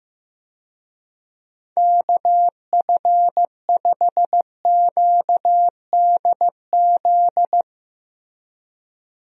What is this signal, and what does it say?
Morse code, heard as a keyed tone, about 15 wpm, 700 Hz. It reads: KF5QDZ
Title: USA Call Signs - 6 Characters - 15wpm